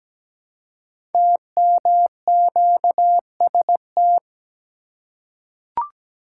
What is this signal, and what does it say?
Morse code, heard as a keyed tone, about 17 wpm, 700 Hz. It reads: TMQST
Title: Code Groups - 5 Letter:Numbers - 17wpm